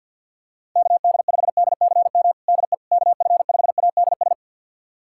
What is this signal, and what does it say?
Morse code, heard as a keyed tone, about 51 wpm, 700 Hz. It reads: KD5DCM DE KF5ADS
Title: USA Callsign DE Callsign 50wpm